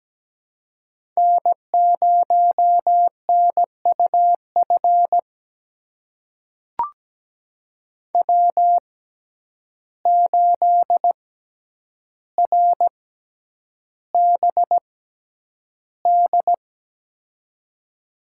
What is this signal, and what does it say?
Morse code, heard as a keyed tone, about 17 wpm, spaced as for 9 wpm, 700 Hz. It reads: N0NUF W8RBD
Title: USA Call Signs - 5 Characters - 6x Character Spacing - 17wpm